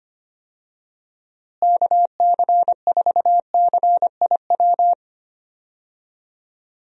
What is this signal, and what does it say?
Morse code, heard as a keyed tone, about 25 wpm, 700 Hz. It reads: KC4CIW
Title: USA Call Signs - 6 Characters - Repeated 2x - 25wpm